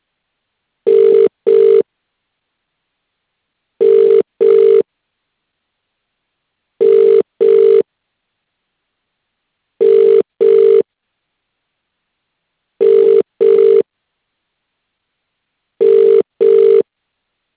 Ringback tone (UK), 400 + 450 Hz.